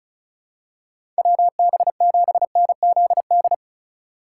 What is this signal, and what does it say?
Morse code, heard as a keyed tone, about 34 wpm, 700 Hz. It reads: WB7NZD
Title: USA Call Signs - 6 Characters - 2x Character Spacing - 35wpm